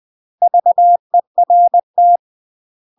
Morse code, heard as a keyed tone, about 20 wpm, 700 Hz. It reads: VERT
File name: UK MASTER QSOs TRAINING/20WPM